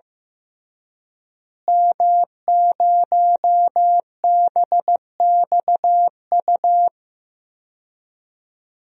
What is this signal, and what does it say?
Morse code, heard as a keyed tone, about 15 wpm, 700 Hz. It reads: M0BXU
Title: DXCC-100 Call Signs - 5 Characters - Repeated 3x - 15wpm